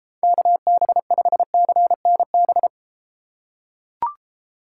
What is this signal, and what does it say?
Morse code, heard as a keyed tone, about 33 wpm, 700 Hz. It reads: KB5CNB